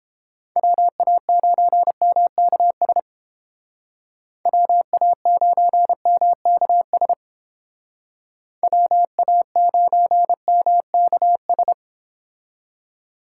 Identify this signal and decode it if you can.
Morse code, heard as a keyed tone, about 30 wpm, 700 Hz. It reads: WA9MKS WA9MKS WA9MKS